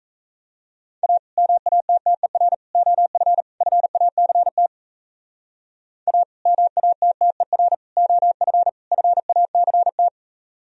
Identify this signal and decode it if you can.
Morse code, heard as a keyed tone, about 35 wpm, 700 Hz. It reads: A MATTER OF FACT A MATTER OF FACT